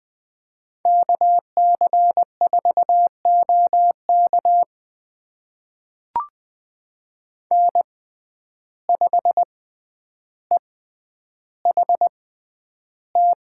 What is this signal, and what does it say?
Morse code, heard as a keyed tone, about 20 wpm, spaced as for 10 wpm, 700 Hz. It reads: KC4OK N5EHT
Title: USA Call Signs - 6x Character Spacing - 20wpm